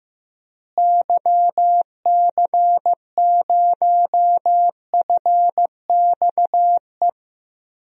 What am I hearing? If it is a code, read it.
Morse code, heard as a keyed tone, about 15 wpm, 700 Hz. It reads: YC0FXE
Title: DXCC-100 Call Signs - 6 Characters - Repeated 2x - 15wpm